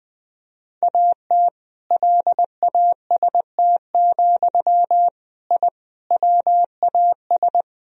Morse code, heard as a keyed tone, about 20 wpm, 700 Hz. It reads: AT LAST, I WAS
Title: Sentences from Top 1000 Words - Review Part 1 - 20wpm